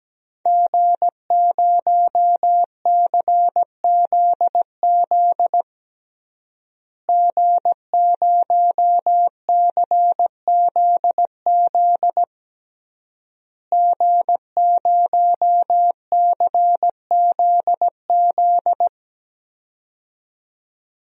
Morse code, heard as a keyed tone, about 17 wpm, 700 Hz. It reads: G0CZZ G0CZZ G0CZZ
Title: DXCC-100 Call Signs - 5 Characters - Repeated 3x - 17wpm